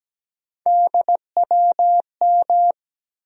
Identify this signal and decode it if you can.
Morse code, heard as a keyed tone, about 17 wpm, 700 Hz. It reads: DWM
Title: USA Call Sign Postfixes - 3 Letters 17wpm